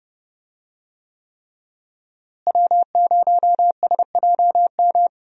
Morse code, heard as a keyed tone, about 30 wpm, 700 Hz. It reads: W0SJM
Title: USA Call Signs - 5 Characters - Repeated 3x - 30wpm